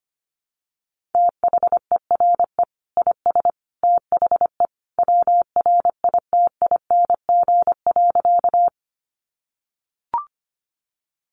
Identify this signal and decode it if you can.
Morse code, heard as a keyed tone, about 25 wpm, 700 Hz. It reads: THERE IS THE WRITING.
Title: Sentences from Top 800 Words 25wpm